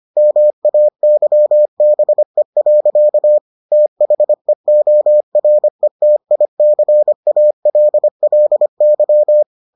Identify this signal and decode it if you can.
Morse code, heard as a keyed tone, about 25 wpm, 600 Hz. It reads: MAYBE. THEORETICALLY